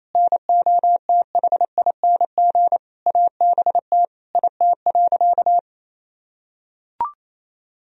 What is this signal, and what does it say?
Morse code, heard as a keyed tone, about 28 wpm, 700 Hz. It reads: NOTHING ABT IT.
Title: Sentences from Top 500 Words and Abbreviations 28wpm